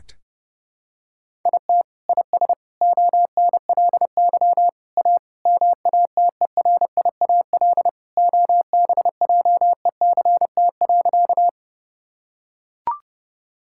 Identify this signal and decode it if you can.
Morse code, heard as a keyed tone, about 30 wpm, 700 Hz. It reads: IT IS ONLY A MATERIAL OBJECT.
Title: Sentences from Top 1000 Words - Review Part 5 - 30wpm